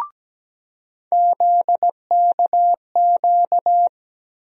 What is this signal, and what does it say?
Morse code, heard as a keyed tone, about 17 wpm, 700 Hz. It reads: ZKQ
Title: USA Call Sign Postfixes - 3 Letters 17wpm